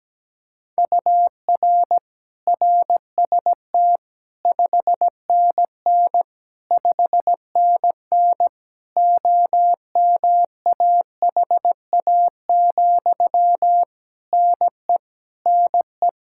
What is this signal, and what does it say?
Morse code, heard as a keyed tone, about 17 wpm, 700 Hz. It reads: UR RST 5NN 5NN OMAHA, NE NE
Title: Practice QSO - Exchange 1 - Beginning - 17wpm